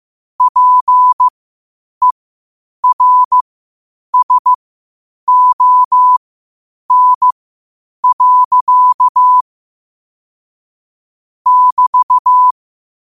Morse code, heard as a keyed tone, about 15 wpm, spaced as for 8 wpm, 1000 Hz. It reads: PERSON. <BT>
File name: 8wpm